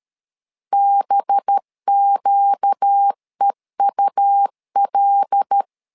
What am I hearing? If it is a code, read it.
Morse code, heard as a keyed tone, about 13 wpm, 800 Hz. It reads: BQEUL
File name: LEZIONI / 5 LBVC / 5 LB 12wpm 1906